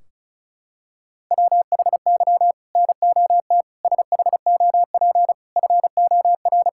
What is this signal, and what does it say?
Morse code, heard as a keyed tone, about 36 wpm, 700 Hz. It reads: WHY NOT SHOP FOR
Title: Sentences from Top 1600 Words 35wpm